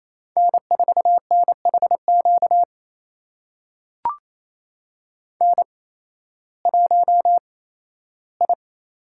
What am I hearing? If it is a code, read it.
Morse code, heard as a keyed tone, about 28 wpm, spaced as for 15 wpm, 700 Hz. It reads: N4NHQ N1I
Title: USA Call Signs - 5 Characters - 8x Character Spacing - 28wpm